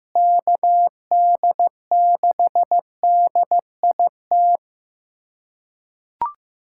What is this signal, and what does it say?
Morse code, heard as a keyed tone, about 15 wpm, 700 Hz. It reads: KD6DIT
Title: USA Callsign DE Callsign - 6 Characters - 15wpm